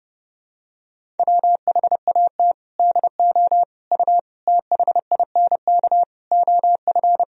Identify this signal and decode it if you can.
Morse code, heard as a keyed tone, about 32 wpm, 700 Hz. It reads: WHAT DO U THINK OF
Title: Sentences from Top 200 Words and Abbreviations 30wpm